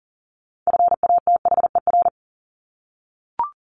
Morse code, heard as a keyed tone, about 40 wpm, 700 Hz. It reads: FATHER
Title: Top 500 Words 40wpm